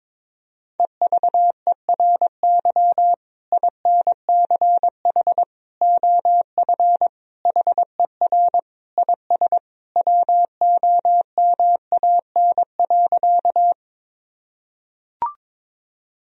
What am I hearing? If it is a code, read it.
Morse code, heard as a keyed tone, about 22 wpm, 700 Hz. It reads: EVERY INCH OF HER IS WOMAN.
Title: Sentences from Top 1000 Words - Review Part 3 - 22wpm